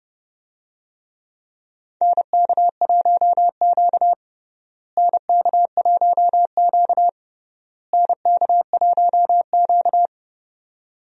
Morse code, heard as a keyed tone, about 32 wpm, 700 Hz. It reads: NK1Q NK1Q NK1Q
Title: USA Call Signs - 4 Characters - Repeated 3x - 30wpm